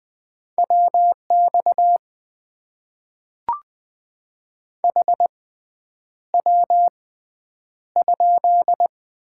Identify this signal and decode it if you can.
Morse code, heard as a keyed tone, about 20 wpm, spaced as for 9 wpm, 700 Hz. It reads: WX HW?
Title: Top 90 QSO Elements - 6x Character Spacing - 20wpm